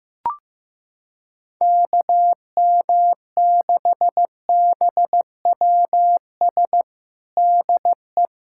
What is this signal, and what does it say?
Morse code, heard as a keyed tone, about 15 wpm, 700 Hz. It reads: KM6BWS DE
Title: USA Callsign DE Callsign - 6 Characters - 15wpm